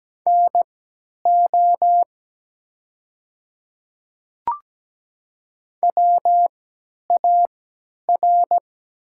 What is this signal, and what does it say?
Morse code, heard as a keyed tone, about 17 wpm, spaced as for 10 wpm, 700 Hz. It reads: NO WAR
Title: 3x Character Spacing - Top 500 Words - 17wpm